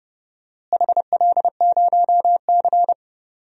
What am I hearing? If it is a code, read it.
Morse code, heard as a keyed tone, about 30 wpm, 700 Hz. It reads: HL0C
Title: DXCC-100 Call Signs - 4 Characters - Repeated 3x - 30wpm